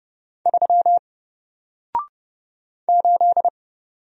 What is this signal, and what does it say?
Morse code, heard as a keyed tone, about 30 wpm, 700 Hz. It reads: 3 8
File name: Single-Letter-Number-30wpm